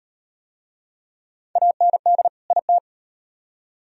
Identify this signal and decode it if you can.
Morse code, heard as a keyed tone, about 38 wpm, 700 Hz. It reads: AND IT